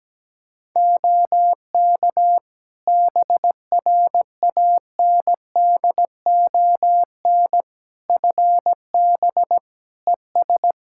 Morse code, heard as a keyed tone, about 17 wpm, 700 Hz. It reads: OK BRANDON FB ES
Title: Practice QSO - Exchange 2 - Info - 17wpm